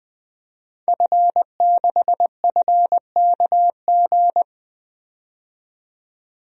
Morse code, heard as a keyed tone, about 20 wpm, 700 Hz. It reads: F6FKG
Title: DXCC-100 Call Signs - 5 Characters - 20wpm